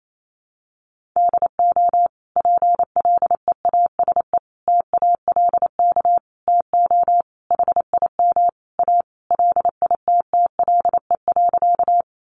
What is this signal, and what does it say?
Morse code, heard as a keyed tone, about 28 wpm, 700 Hz. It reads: DO PLEASE TALK TO HIM A LITTLE.
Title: Sentences from Top 1500 Words 28wpm